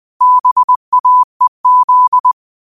Morse code, heard as a keyed tone, about 20 wpm, 1000 Hz. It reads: BAEZ
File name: qotdfiles / 20wpm